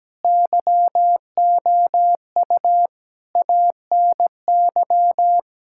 Morse code, heard as a keyed tone, about 17 wpm, 700 Hz. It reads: YOU ANY
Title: Sentences from Top 1600 Words 17wpm